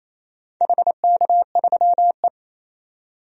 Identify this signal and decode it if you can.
Morse code, heard as a keyed tone, about 28 wpm, 700 Hz. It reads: HK3E
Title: DXCC-100 Call Signs - 4 Characters - 28wpm